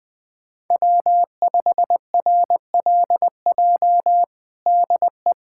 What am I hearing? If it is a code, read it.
Morse code, heard as a keyed tone, about 21 wpm, 700 Hz. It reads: W5RLJ DE
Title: USA Callsign DE Callsign - 5 Characters - 20wpm